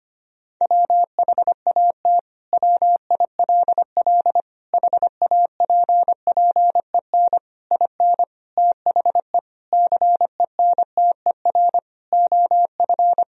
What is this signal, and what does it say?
Morse code, heard as a keyed tone, about 25 wpm, 700 Hz. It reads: WHAT WILL HAPPEN IN THE CENTER OF
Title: Sentences from Top 1000 Words - Review Part 2 - 25wpm